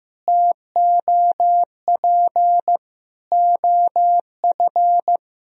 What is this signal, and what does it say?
Morse code, heard as a keyed tone, about 15 wpm, 700 Hz. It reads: TOP OF
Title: Sentences from Top 1000 Words - Review Part 2 - 15wpm